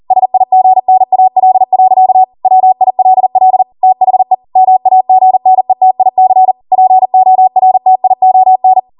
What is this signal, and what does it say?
Morse code, heard as a keyed tone, about 40 wpm, 750 Hz. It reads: SIGNAL. WILL THE MAGNETIC PORTION
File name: Morse / 40WPM